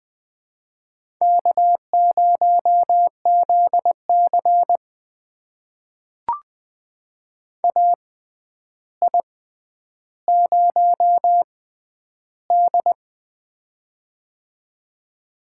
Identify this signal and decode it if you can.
Morse code, heard as a keyed tone, about 20 wpm, spaced as for 10 wpm, 700 Hz. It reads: K0ZC AI0D
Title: USA Call Signs - 4 Characters - 6x Character Spacing - 20wpm